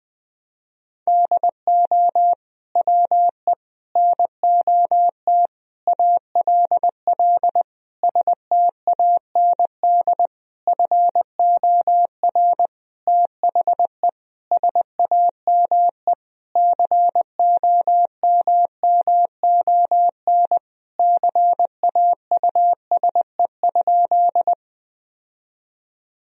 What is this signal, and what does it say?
Morse code, heard as a keyed tone, about 20 wpm, 700 Hz. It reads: DO WE NOT ALL STAND FOR THE SAME COMMON CAUSE?
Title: Sentences from Top 1000 Words - Review Part 3 - 20wpm